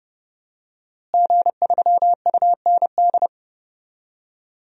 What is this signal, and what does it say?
Morse code, heard as a keyed tone, about 30 wpm, 700 Hz. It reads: G3UND
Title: DXCC-100 Call Signs - 5 Characters - Repeated 3x - 30wpm